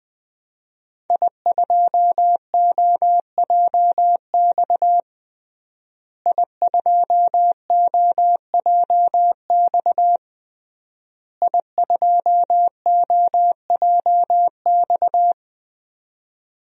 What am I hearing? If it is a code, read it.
Morse code, heard as a keyed tone, about 20 wpm, 700 Hz. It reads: I2OJX I2OJX I2OJX
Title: DXCC-100 Call Signs - 5 Characters - Repeated 3x - 20wpm